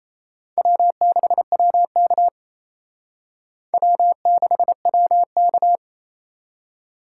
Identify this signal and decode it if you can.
Morse code, heard as a keyed tone, about 35 wpm, 700 Hz. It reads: W6WK W6WK
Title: Speed Racing - USA 1x2 Call Signs 28wpm